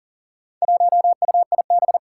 Morse code, heard as a keyed tone, about 40 wpm, 700 Hz. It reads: 1UIB
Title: Four Challenging Letter-Number Combinations 40wpm